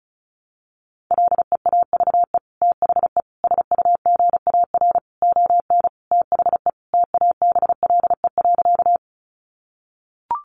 Morse code, heard as a keyed tone, about 34 wpm, 700 Hz. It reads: LEAVE THE SUGAR ON THE TABLE.
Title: Sentences from Top 1200 Words 35wpm